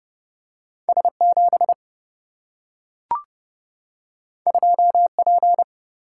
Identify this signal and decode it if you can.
Morse code, heard as a keyed tone, about 30 wpm, 700 Hz. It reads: S 7 2 P
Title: Single-Letter Single-Number 30wpm